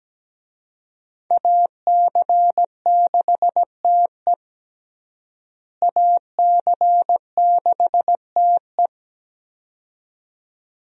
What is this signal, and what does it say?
Morse code, heard as a keyed tone, about 17 wpm, 700 Hz. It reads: AC6TE AC6TE
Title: USA Call Signs - 5 Characters - Repeated 2x - 17wpm